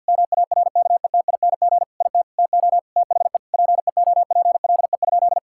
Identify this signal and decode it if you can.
Morse code, heard as a keyed tone, about 50 wpm, 700 Hz. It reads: MARKETING IT TO THE PEOPLE?
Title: Sentences from Top 1000 Words - Review Part 5 - 50wpm